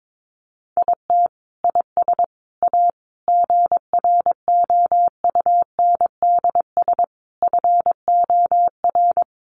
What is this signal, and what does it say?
Morse code, heard as a keyed tone, about 22 wpm, 700 Hz. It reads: IT IS A GROUNDS FOR
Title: Sentences from Top 1300 Words 22wpm